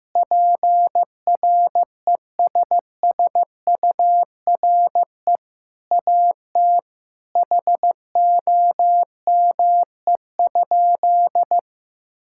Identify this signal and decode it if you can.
Morse code, heard as a keyed tone, about 15 wpm, 700 Hz. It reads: PRESSURE AT HOME?
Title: Sentences from Top 1000 Words - Review Part 6 - 15wpm